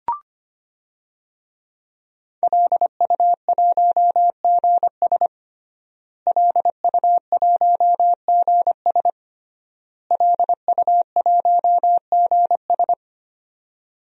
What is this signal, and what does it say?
Morse code, heard as a keyed tone, about 25 wpm, 700 Hz. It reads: LU1GS LU1GS LU1GS
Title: DXCC-100 Call Signs - 5 Characters - Repeated 3x - 25wpm